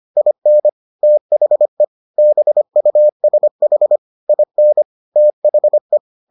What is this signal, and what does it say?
Morse code, heard as a keyed tone, about 25 wpm, 600 Hz. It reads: IN THE BUSH IN THE